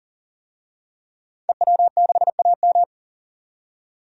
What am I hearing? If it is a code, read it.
Morse code, heard as a keyed tone, about 40 wpm, 700 Hz. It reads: EW6AM
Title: DXCC-100 Call Signs - 5 Characters - 40wpm